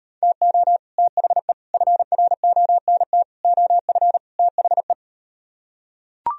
Morse code, heard as a keyed tone, about 38 wpm, 700 Hz. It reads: TO THE FRONT OF THE